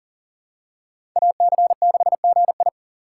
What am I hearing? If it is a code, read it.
Morse code, heard as a keyed tone, about 40 wpm, 700 Hz. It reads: AC6GI
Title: USA Call Signs - 4x Character Spacing - 40wpm